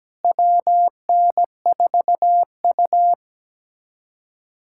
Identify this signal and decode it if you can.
Morse code, heard as a keyed tone, about 17 wpm, 700 Hz. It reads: WN4U
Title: USA Callsign DE Callsign - 4 Characters - 17wpm